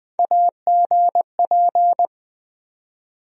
Morse code, heard as a keyed tone, about 20 wpm, 700 Hz. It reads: AGP
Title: USA Call Sign Postfixes - 3 Letters 20wpm